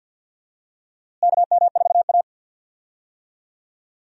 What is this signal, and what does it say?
Morse code, heard as a keyed tone, about 52 wpm, 700 Hz. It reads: KM4A
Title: USA Call Signs - 4 Characters - Repeated 2x - 50wpm